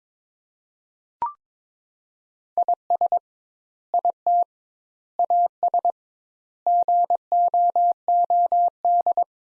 Morse code, heard as a keyed tone, about 22 wpm, 700 Hz. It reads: IS IT AS GOOD
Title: Sentences Top 200 Words - 22wpm